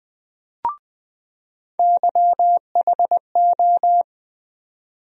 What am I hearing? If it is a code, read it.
Morse code, heard as a keyed tone, about 21 wpm, 700 Hz. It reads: YHO
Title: USA Call Sign Postfixes - 3 Letters 20wpm